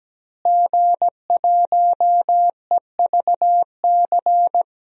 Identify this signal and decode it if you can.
Morse code, heard as a keyed tone, about 17 wpm, 700 Hz. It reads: G1EVC
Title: DXCC-100 Call Signs - 5 Characters - Repeated 3x - 17wpm